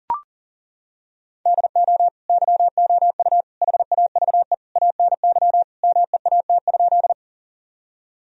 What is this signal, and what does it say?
Morse code, heard as a keyed tone, about 40 wpm, 700 Hz. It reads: DO YOU HAVE ANY MEAT?
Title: Sentences from Top 1000 Words - Review Part 3 - 40wpm